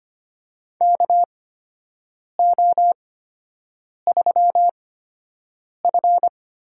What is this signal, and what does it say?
Morse code, heard as a keyed tone, about 25 wpm, spaced as for 7 wpm, 700 Hz. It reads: KO3F
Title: USA Call Signs - 4 Characters - 8x Character Spacing - 25wpm